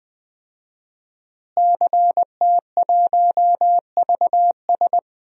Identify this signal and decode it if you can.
Morse code, heard as a keyed tone, about 20 wpm, 700 Hz. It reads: CT1VS
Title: DXCC-100 Call Signs - 5 Characters - Repeated 2x - 20wpm